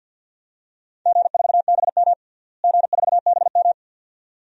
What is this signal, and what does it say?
Morse code, heard as a keyed tone, about 52 wpm, 700 Hz. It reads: G4BM G4BM
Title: DXCC-100 Call Signs - 4 Characters - Repeated 2x - 50wpm